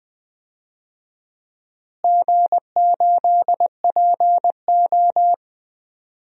Morse code, heard as a keyed tone, about 20 wpm, 700 Hz. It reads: G8PO
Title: DXCC-100 Call Signs - 4 Characters - Repeated 3x - 20wpm